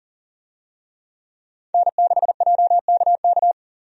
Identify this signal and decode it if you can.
Morse code, heard as a keyed tone, about 40 wpm, 700 Hz. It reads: N6JKK
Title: USA Call Signs - 5 Characters - Repeated 3x - 40wpm